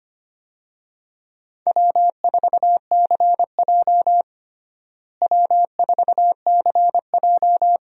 Morse code, heard as a keyed tone, about 25 wpm, 700 Hz. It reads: W4CJ W4CJ
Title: USA Call Signs - 4 Characters - Repeated 2x - 25wpm